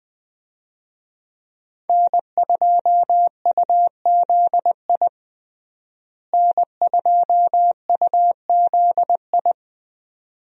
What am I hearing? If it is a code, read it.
Morse code, heard as a keyed tone, about 20 wpm, 700 Hz. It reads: N2UZI N2UZI
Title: USA Call Signs - 5 Characters - Repeated 2x - 20wpm